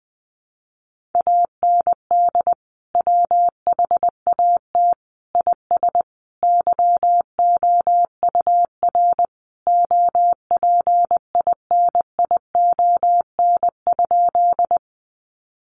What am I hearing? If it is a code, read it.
Morse code, heard as a keyed tone, about 20 wpm, 700 Hz. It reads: AND WHAT IS YOUR OPINION?
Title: Sentences from Top 1200 Words 20wpm